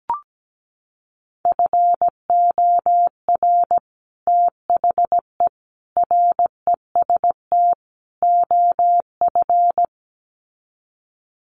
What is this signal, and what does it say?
Morse code, heard as a keyed tone, about 17 wpm, 700 Hz. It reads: FOR THE REST OF
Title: Sets of 4 Words 17wpm